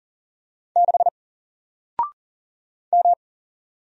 Morse code, heard as a keyed tone, about 39 wpm, 700 Hz. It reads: 6 M
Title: Single Letter-Number Alt 40wpm